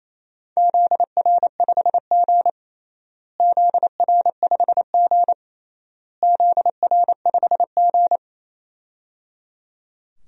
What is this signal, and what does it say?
Morse code, heard as a keyed tone, about 28 wpm, 700 Hz. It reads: ZR5G ZR5G ZR5G
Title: DXCC-100 Call Signs - 4 Characters - Repeated 3x - 28wpm